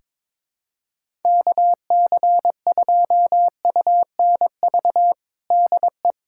Morse code, heard as a keyed tone, about 22 wpm, 700 Hz. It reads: KC2UNV DE
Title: USA Callsign DE Callsign - 6 Characters - 22wpm